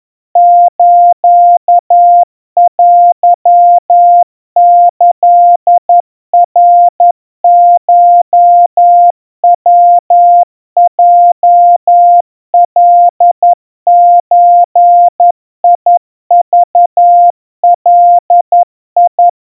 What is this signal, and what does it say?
Morse code (Japanese Wabun code), heard as a keyed tone, about 11 wpm, 700 Hz. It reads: ステキナコヤヲカゾクガ
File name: kawa / Morse sample / kamyu 40 cpm 700 Hz Crispy